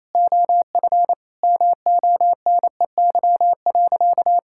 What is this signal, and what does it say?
Morse code, heard as a keyed tone, about 28 wpm, 700 Hz. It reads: OF MONEY.
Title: Sentences from Top 1000 Words - Review Part 6 - 28wpm